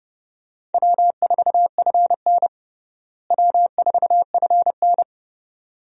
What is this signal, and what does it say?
Morse code, heard as a keyed tone, about 32 wpm, 700 Hz. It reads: W4FN W4FN
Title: USA Call Signs - 4 Characters - Repeated 2x - 30wpm